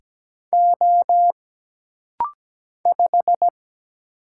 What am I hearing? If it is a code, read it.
Morse code, heard as a keyed tone, about 17 wpm, 700 Hz. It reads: O 5
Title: Single Letter-Number Alt - Rapid-Fire 17wpm